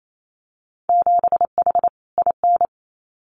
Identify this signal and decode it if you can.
Morse code, heard as a keyed tone, about 28 wpm, 700 Hz. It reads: 7H IN
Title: Winter Field Day Exchanges Encore 28wpm